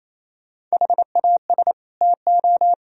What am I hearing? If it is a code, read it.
Morse code, heard as a keyed tone, about 34 wpm, 700 Hz. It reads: HAS TO
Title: Speed-Racing - Sets of 2 Words 28wpm